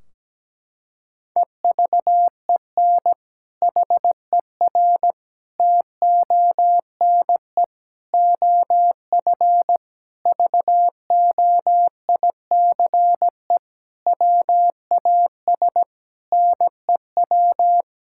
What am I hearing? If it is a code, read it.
Morse code, heard as a keyed tone, about 17 wpm, 700 Hz. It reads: EVEN HER TONE OF VOICE WAS NEW
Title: Sentences from Top 1600 Words 17wpm